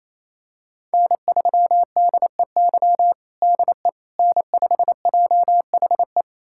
Morse code, heard as a keyed tone, about 28 wpm, 700 Hz. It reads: N3DEY DE N5JHE
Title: USA Callsign DE Callsign - 5 Characters - 28wpm